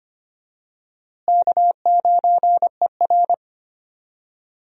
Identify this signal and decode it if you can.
Morse code, heard as a keyed tone, about 25 wpm, 700 Hz. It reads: K9ER
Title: USA Call Signs - 4 Characters - Repeated 3x - 25wpm